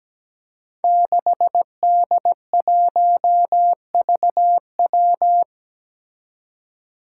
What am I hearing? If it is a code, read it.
Morse code, heard as a keyed tone, about 17 wpm, 700 Hz. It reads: 6D1VW
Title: Five Challenging Letter-Number Combinations 17wpm